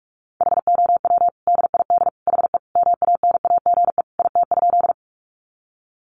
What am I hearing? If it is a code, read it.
Morse code, heard as a keyed tone, about 44 wpm, 700 Hz. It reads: HOW DID HE MANAGE IT?
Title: Sentences from Top 1400 Words 45wpm